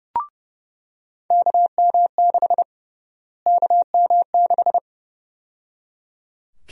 Morse code, heard as a keyed tone, about 30 wpm, 700 Hz. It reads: KM6 KM6
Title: USA Call Sign Prefixes - Repeated 2x - 30wpm